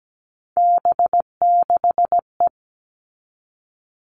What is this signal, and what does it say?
Morse code, heard as a keyed tone, about 17 wpm, 700 Hz. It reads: B6E
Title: All Cut Numbers - 3-Digit 17wpm